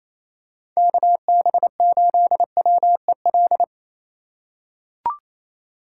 Morse code, heard as a keyed tone, about 28 wpm, 700 Hz. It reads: KB8WEL
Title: USA Call Signs - 6 Characters - 4x Character Spacing - 28wpm